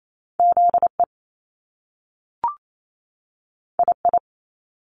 Morse code, heard as a keyed tone, about 28 wpm, 700 Hz. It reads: ZE II
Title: Top Two Letter Combinations in Words 28wpm